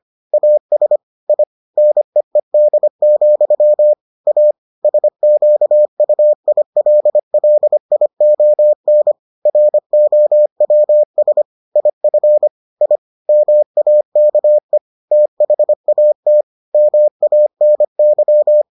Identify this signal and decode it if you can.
Morse code, heard as a keyed tone, about 25 wpm, 600 Hz. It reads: AS I NEED, A SQUILLION ROWS IF I MAKE THAT MANY